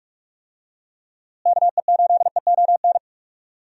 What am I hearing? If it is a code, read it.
Morse code, heard as a keyed tone, about 46 wpm, 700 Hz. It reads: KE8EON
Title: USA Call Signs - 6 Characters - Repeated 2x - 45wpm